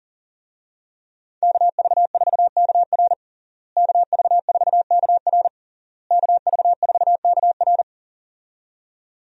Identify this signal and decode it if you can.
Morse code, heard as a keyed tone, about 40 wpm, 700 Hz. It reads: KV4KR KV4KR KV4KR
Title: USA Call Signs - 5 Characters - Repeated 3x - 40wpm